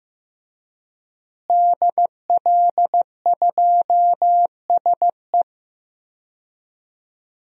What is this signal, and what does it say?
Morse code, heard as a keyed tone, about 15 wpm, 700 Hz. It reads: DL2SE
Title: DXCC-100 Call Signs - 5 Characters - Repeated 2x - 15wpm